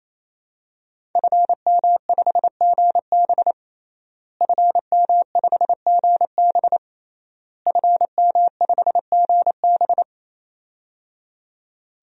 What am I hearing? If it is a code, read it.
Morse code, heard as a keyed tone, about 28 wpm, 700 Hz. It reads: FM5GB FM5GB FM5GB
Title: DXCC-100 Call Signs - 5 Characters - Repeated 3x - 28wpm